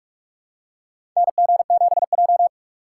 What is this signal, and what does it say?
Morse code, heard as a keyed tone, about 46 wpm, 700 Hz. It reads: NG7J
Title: USA Call Signs - 4 Characters - 4x Character Spacing - 45wpm